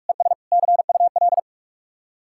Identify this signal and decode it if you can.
Morse code, heard as a keyed tone, about 50 wpm, 700 Hz. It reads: ES CUL